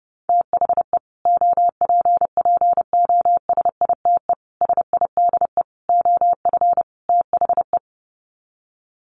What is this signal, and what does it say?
Morse code, heard as a keyed tone, about 30 wpm, 700 Hz. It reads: THE OPPOSITE SIDE OF THE
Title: Sets of 5 Words Encore 30wpm